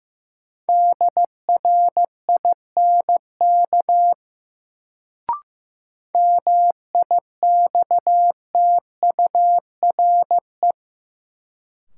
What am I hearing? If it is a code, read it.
Morse code, heard as a keyed tone, about 16 wpm, 700 Hz. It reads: DRINK MIXTURE
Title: Sentences from Top 1500 Words - New Word Drill - 15wpm